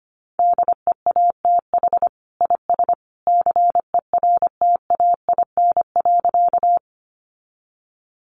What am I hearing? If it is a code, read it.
Morse code, heard as a keyed tone, about 25 wpm, 700 Hz. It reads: DEATH IS CERTAIN.
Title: Sentences from Top 1100 Words 25wpm